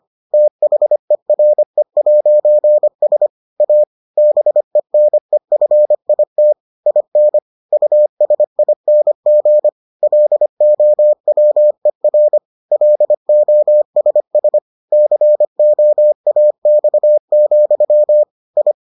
Morse code, heard as a keyed tone, about 25 wpm, 600 Hz. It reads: THERE'S A BENEFIT IN USING LOWER LOSS COAX, I